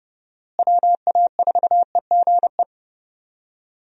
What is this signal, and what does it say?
Morse code, heard as a keyed tone, about 30 wpm, 700 Hz. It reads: WA4EGE
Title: USA Call Signs - 6 Characters - Repeated 3x - 30wpm